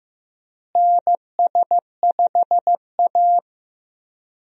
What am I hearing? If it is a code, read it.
Morse code, heard as a keyed tone, about 15 wpm, 700 Hz. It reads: NS5A
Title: USA Call Signs - 4 Characters - Repeated 3x - 15wpm